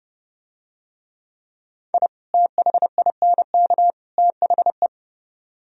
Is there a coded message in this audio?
Yes: Morse code from 1.9 to 4.9 s, heard as a keyed tone, about 30 wpm, 700 Hz. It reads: I THINK THE